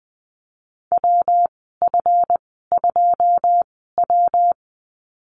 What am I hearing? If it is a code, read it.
Morse code, heard as a keyed tone, about 20 wpm, spaced as for 15 wpm, 700 Hz. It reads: WF2W
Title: 161 Lesson 26g - Letter g - Call Signs 20wpm